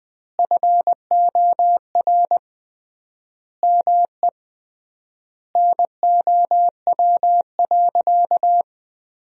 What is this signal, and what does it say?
Morse code, heard as a keyed tone, about 20 wpm, 700 Hz. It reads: FOR ME NOW.
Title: 3x Word-Spacing - Sentences from Top 200 Words - 20wpm